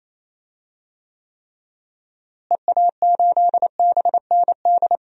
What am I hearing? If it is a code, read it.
Morse code, heard as a keyed tone, about 28 wpm, 700 Hz. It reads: EA8BND
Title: DXCC-100 Call Signs - 6 Characters - Repeated 3x - 28wpm